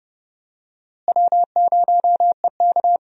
Morse code, heard as a keyed tone, about 32 wpm, 700 Hz. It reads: W0EK
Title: USA Call Signs - 4 Characters - Repeated 3x - 30wpm